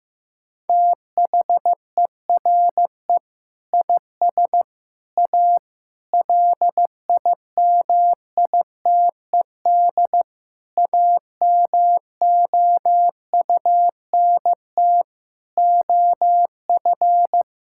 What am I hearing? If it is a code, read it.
Morse code, heard as a keyed tone, about 16 wpm, 700 Hz. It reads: THERE IS A LIMITED AMOUNT OF
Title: Sentences from Top 1000 Words - Review Part 6 - 15wpm